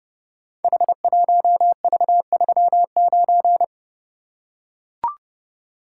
Morse code, heard as a keyed tone, about 30 wpm, 700 Hz. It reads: H1V39
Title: Code Groups - 5 Letter:Numbers - 30wpm